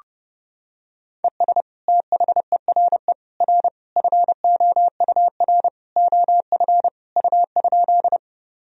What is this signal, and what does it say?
Morse code, heard as a keyed tone, about 30 wpm, 700 Hz. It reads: ES THERE R FOUR OF U?
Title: Sentences from Top 300 Words and Abbreviations 30wpm